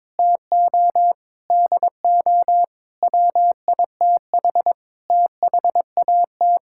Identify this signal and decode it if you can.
Morse code, heard as a keyed tone, about 22 wpm, 700 Hz. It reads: TO DO WITH THAT
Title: Sentences from Top 1000 Words - Review Part 5 - 22wpm